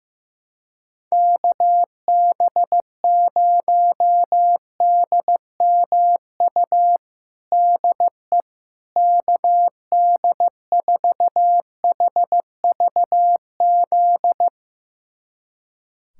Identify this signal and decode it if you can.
Morse code, heard as a keyed tone, about 15 wpm, 700 Hz. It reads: KB0DMU DE KD4HVZ
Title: USA Callsign DE Callsign 15wpm